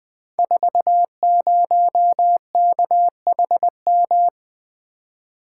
Morse code, heard as a keyed tone, about 21 wpm, 700 Hz. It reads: 40KHM
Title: Code Groups - 5 Letter:Numbers - 20wpm